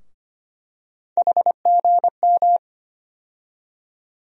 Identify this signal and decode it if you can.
Morse code, heard as a keyed tone, about 25 wpm, 700 Hz. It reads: HGM
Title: USA Call Sign Postfixes - 3 Letters 25wpm